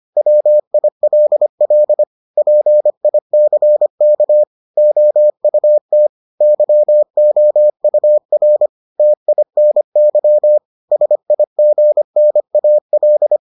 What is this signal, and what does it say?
Morse code, heard as a keyed tone, about 25 wpm, 600 Hz. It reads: WILL PICK OUT YOUR TINY SIGNAL